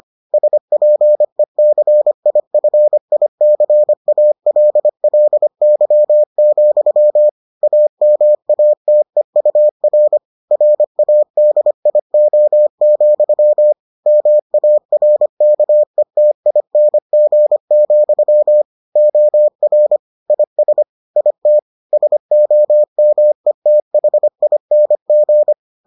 Morse code, heard as a keyed tone, about 25 wpm, 600 Hz. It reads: SPECIFICALLY, AMATEUR RADIO, MARKETING, OR IS IT SOMETHING